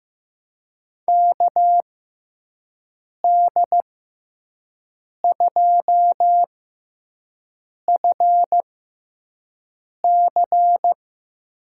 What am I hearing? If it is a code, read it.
Morse code, heard as a keyed tone, about 15 wpm, spaced as for 5 wpm, 700 Hz. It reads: KD2FC
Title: USA Call Signs - 6x Character Spacing - 15wpm